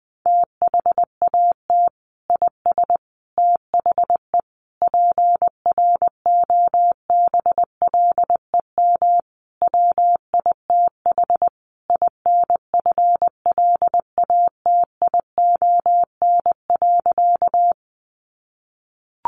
Morse code, heard as a keyed tone, about 20 wpm, 700 Hz. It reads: THAT IS THE PROBLEM WITH INFLATION.
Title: Sentences from Top 1100 Words 20wpm